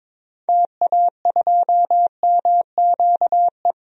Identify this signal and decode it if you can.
Morse code, heard as a keyed tone, about 22 wpm, 700 Hz. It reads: TA2MQE
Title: DXCC-100 Call Signs - 6 Characters - 22wpm